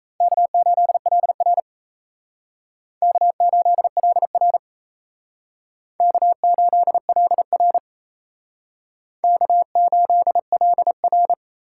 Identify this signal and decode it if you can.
Morse code, heard as a keyed tone, about 33 wpm, 700 Hz. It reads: K8LR K8LR K8LR K8LR
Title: Speed Racing - USA 1x2 Call Signs 28wpm